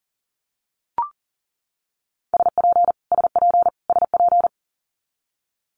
Morse code, heard as a keyed tone, about 38 wpm, 700 Hz. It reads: SP SP SP
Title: Multi-letter Phonemes Repeated 3x 40wpm